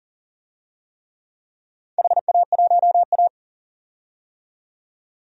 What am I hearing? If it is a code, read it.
Morse code, heard as a keyed tone, about 40 wpm, 700 Hz. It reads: HA1A
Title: DXCC-100 Call Signs - 4 Characters - 40wpm